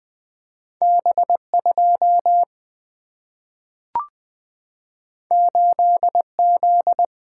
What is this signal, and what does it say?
Morse code, heard as a keyed tone, about 20 wpm, 700 Hz. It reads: B2 8Z
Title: Single-Letter Single-Number 20wpm